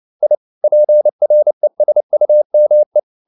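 Morse code, heard as a keyed tone, about 29 wpm, 600 Hz. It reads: I PRESUME
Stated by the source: Morse code at 29 wpm, sending Ken Follett's novel Never in English, from CW med Gnister.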